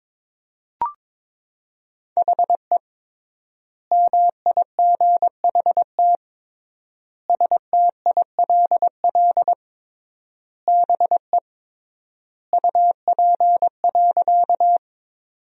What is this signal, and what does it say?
Morse code, heard as a keyed tone, about 23 wpm, 700 Hz. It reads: HE MIGHT STILL BE UP.
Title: 3x Word-Spacing - Sentences from Top 300 Words - 22wpm